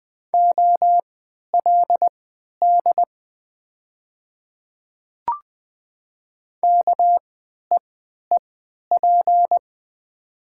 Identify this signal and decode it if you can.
Morse code, heard as a keyed tone, about 20 wpm, spaced as for 11 wpm, 700 Hz. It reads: OLD KEEP
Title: 3x Character Spacing - Top 500 Words - 20wpm